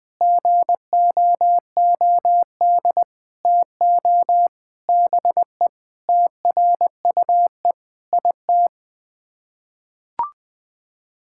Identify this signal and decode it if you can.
Morse code, heard as a keyed tone, about 21 wpm, 700 Hz. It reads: GOOD TO BE TRUE IT
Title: Sets of 9 Words - 20wpm